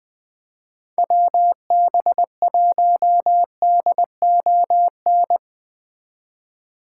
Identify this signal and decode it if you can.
Morse code, heard as a keyed tone, about 20 wpm, 700 Hz. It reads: WB1DON
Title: Speed Racing - USA 2x3 Call Signs 17wpm